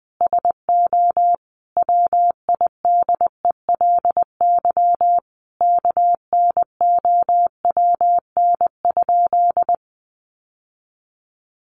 Morse code, heard as a keyed tone, about 20 wpm, 700 Hz. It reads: SO WIDELY KNOWN?
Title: Sentences from Top 1200 Words 20wpm